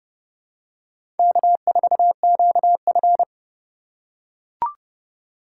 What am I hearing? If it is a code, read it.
Morse code, heard as a keyed tone, about 30 wpm, 700 Hz. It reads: K4QF
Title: USA Call Signs - 8x Character Spacing - 30wpm